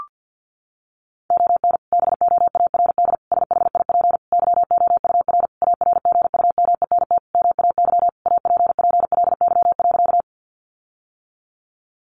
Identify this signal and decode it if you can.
Morse code, heard as a keyed tone, about 50 wpm, 700 Hz. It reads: ON BOARD SHIP YOUR ARGUMENT MAY APPLY.
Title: Sentences from Top 1400 Words 50wpm